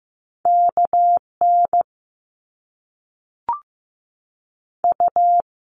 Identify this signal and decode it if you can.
Morse code, heard as a keyed tone, about 15 wpm, 700 Hz. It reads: KN U
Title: Top Two Letter Combinations in Words 15wpm